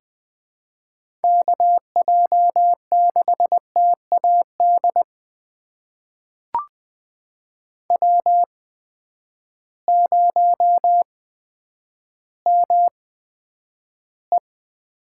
Morse code, heard as a keyed tone, about 20 wpm, spaced as for 10 wpm, 700 Hz. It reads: KJ6TAD W0ME
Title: USA Call Signs - 8x Character Spacing - 20wpm